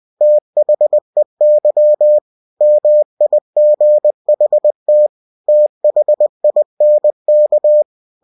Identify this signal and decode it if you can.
Morse code, heard as a keyed tone, about 20 wpm, 600 Hz. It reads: THEY MIGHT THINK
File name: cw-books / winnie-the-pooh / winnie-20wpm-page-0005